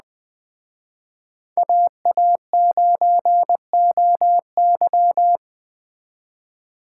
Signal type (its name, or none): Morse code